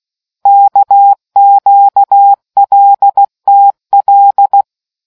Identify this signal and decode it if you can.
Morse code, heard as a keyed tone, about 16 wpm, 800 Hz. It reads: KQLTL